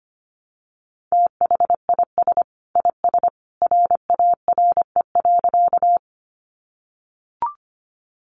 Morse code, heard as a keyed tone, about 25 wpm, 700 Hz. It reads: THIS IS RARE.
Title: Sentences from Top 1100 Words 25wpm